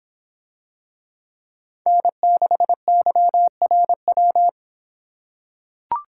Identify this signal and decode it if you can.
Morse code, heard as a keyed tone, about 26 wpm, 700 Hz. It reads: N6YRW